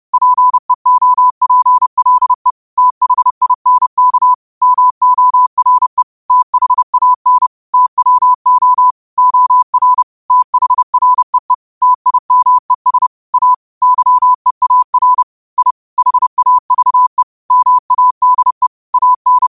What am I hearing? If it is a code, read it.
Morse code, heard as a keyed tone, about 30 wpm, 1000 Hz. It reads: PEOPLE THINK MORE THAN TWO OR THREE TIMES A YEAR I HAVE MADE AN